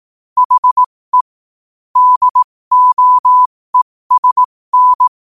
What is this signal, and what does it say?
Morse code, heard as a keyed tone, about 18 wpm, spaced as for 15 wpm, 1000 Hz. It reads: HE DOESN
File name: qotdfiles / 15wpm